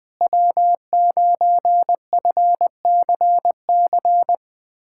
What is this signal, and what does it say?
Morse code, heard as a keyed tone, about 20 wpm, 700 Hz. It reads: W9FCC
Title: USA Callsign DE Callsign - 5 Characters - 20wpm